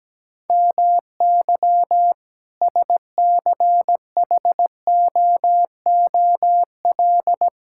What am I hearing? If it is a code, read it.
Morse code, heard as a keyed tone, about 17 wpm, 700 Hz. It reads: MY SCHOOL
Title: Sentences from Top 1000 Words - Review Part 6 - 17wpm